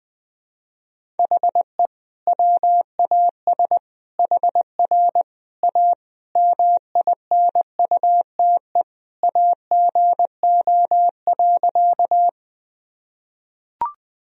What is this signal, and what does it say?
Morse code, heard as a keyed tone, about 20 wpm, 700 Hz. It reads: HE WAS HR A MINUTE AGO.
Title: Sentences from Top 500 Words and Abbreviations 20wpm